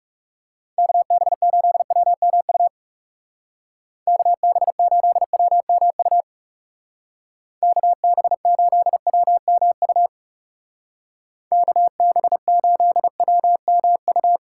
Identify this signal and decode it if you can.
Morse code, heard as a keyed tone, about 36 wpm, 700 Hz. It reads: KB8WMU KB8WMU KB8WMU KB8WMU
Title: Speed Racing - USA 2x3 Call Signs 30wpm